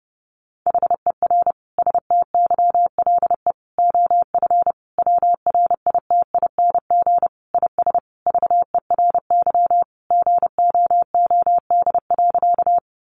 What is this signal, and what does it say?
Morse code, heard as a keyed tone, about 30 wpm, 700 Hz. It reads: HER STYLE OF WRITING IS VERY GOOD.
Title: Sentences from Top 800 Words 30wpm